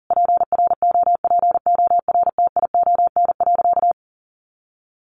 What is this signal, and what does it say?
Morse code, heard as a keyed tone, about 37 wpm, 700 Hz. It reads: PROPORTION.
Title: A Sound Of Thunder 40wpm